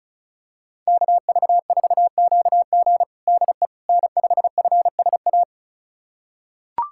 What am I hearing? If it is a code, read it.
Morse code, heard as a keyed tone, about 36 wpm, 700 Hz. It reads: KV4QG DE N5FSA
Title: USA Callsign DE Callsign - 5 Characters - 35wpm